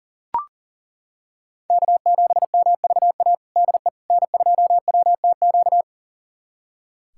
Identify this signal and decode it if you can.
Morse code, heard as a keyed tone, about 40 wpm, 700 Hz. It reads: K7MVA DE N2WTQ
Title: USA Callsign DE Callsign - 5 Characters - 40wpm